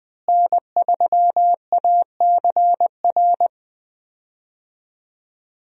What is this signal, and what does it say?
Morse code, heard as a keyed tone, about 20 wpm, 700 Hz. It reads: N3ACR
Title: USA Call Signs - 5 Characters - Repeated 3x - 20wpm